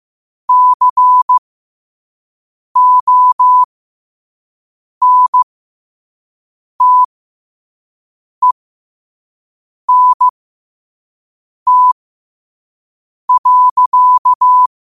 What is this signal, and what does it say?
Morse code, heard as a keyed tone, about 15 wpm, spaced as for 5 wpm, 1000 Hz. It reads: CONTENT.
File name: qotdfiles / 5wpm